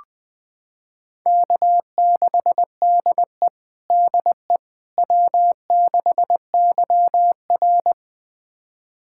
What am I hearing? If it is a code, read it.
Morse code, heard as a keyed tone, about 20 wpm, 700 Hz. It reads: K6DE DE W6YR
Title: USA Callsign DE Callsign - 4 Characters - 20wpm